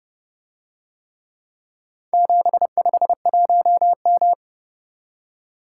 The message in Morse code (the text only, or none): 751M